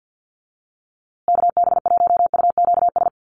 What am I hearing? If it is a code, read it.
Morse code, heard as a keyed tone, about 50 wpm, 700 Hz. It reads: KB1UQS
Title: USA Call Signs Repeated 3x 50wpm